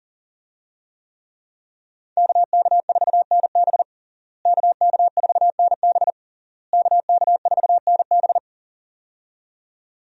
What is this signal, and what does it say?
Morse code, heard as a keyed tone, about 40 wpm, 690 Hz. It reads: KK4NB KK4NB KK4NB
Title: USA Call Signs - 5 Characters - Repeated 3x - 40wpm